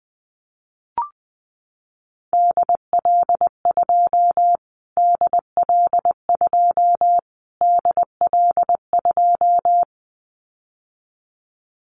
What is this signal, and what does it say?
Morse code, heard as a keyed tone, about 19 wpm, 700 Hz. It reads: DL2 DL2 DL2
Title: Callsign Prefixes for Top 25 DXCC - Repeated 3x - 20wpm